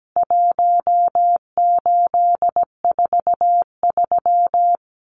Morse code, heard as a keyed tone, about 17 wpm, 700 Hz. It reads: 1843
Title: CWT Exchange 17wpm